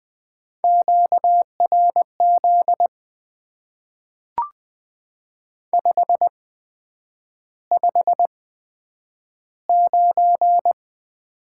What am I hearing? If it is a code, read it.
Morse code, heard as a keyed tone, about 20 wpm, spaced as for 9 wpm, 700 Hz. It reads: QRZ 559
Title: Top 90 QSO Elements - 8x Character Spacing - 20wpm